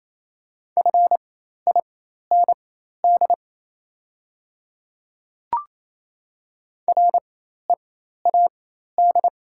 Morse code, heard as a keyed tone, about 27 wpm, spaced as for 13 wpm, 690 Hz. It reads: FIND READ
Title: 4x Character Spacing - Top 300 Words - 28wpm